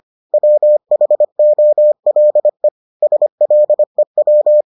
Morse code, heard as a keyed tone, about 25 wpm, 600 Hz. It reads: WHOLE SLEW